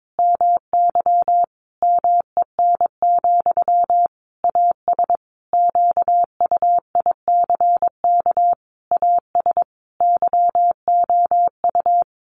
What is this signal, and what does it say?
Morse code, heard as a keyed tone, about 22 wpm, 700 Hz. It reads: MY MEN, AS QUICK AS YOU
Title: Sentences from Top 500 Words 22wpm